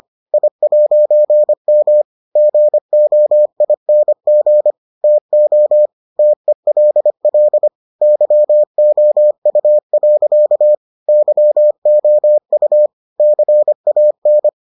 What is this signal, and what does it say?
Morse code, heard as a keyed tone, about 25 wpm, 600 Hz. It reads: I'M GOING TO TELL YOU. YOU CAN